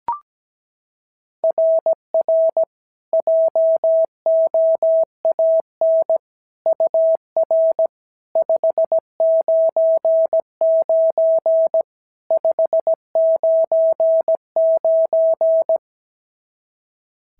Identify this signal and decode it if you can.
Morse code, heard as a keyed tone, about 17 wpm, 650 Hz. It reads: RR JOAN UR 599 599
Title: SOTA QSO Practice 17wpm